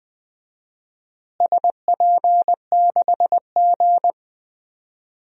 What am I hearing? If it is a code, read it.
Morse code, heard as a keyed tone, about 20 wpm, 700 Hz. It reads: SP6G
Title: DXCC-100 Call Signs - 4 Characters - Repeated 3x - 20wpm